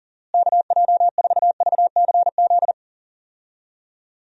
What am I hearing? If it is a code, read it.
Morse code, heard as a keyed tone, about 40 wpm, 700 Hz. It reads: KJ4VCZ
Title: USA Call Signs - 6 Characters - Repeated 2x - 40wpm